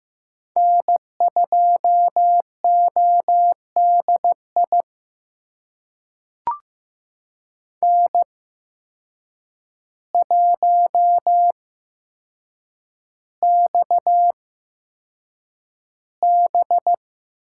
Morse code, heard as a keyed tone, about 15 wpm, spaced as for 7 wpm, 700 Hz. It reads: N2ODI N1XB
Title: USA Call Signs - 5 Characters - 8x Character Spacing - 15wpm